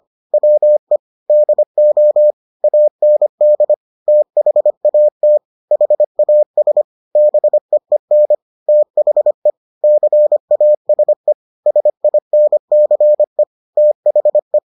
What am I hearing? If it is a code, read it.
Morse code, heard as a keyed tone, about 25 wpm, 600 Hz. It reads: WE DO AND THAT HAS BEEN THE CASE SINCE THE